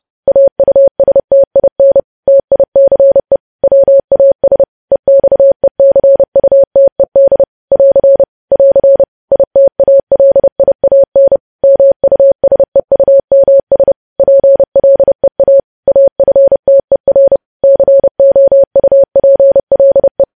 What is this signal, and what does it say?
Morse code, heard as a keyed tone, about 30 wpm, 550 Hz. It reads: AUSTIN TICE WAS EXECUTED <AR> <AR> ITALIAN MUSEUMS PLEA AFTER COUPLE
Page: news headlines in morse code at 30 wpm